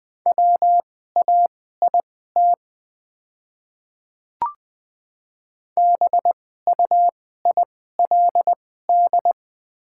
Morse code, heard as a keyed tone, about 20 wpm, spaced as for 15 wpm, 690 Hz. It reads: WAIT BUILD
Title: 2x Character Spacing - Top 500 Words - 20wpm